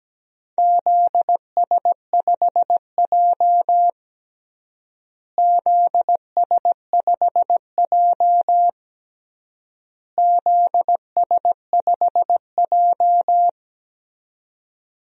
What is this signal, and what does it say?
Morse code, heard as a keyed tone, about 17 wpm, 700 Hz. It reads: ZS5J ZS5J ZS5J
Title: DXCC-100 Call Signs - 4 Characters - Repeated 3x - 17wpm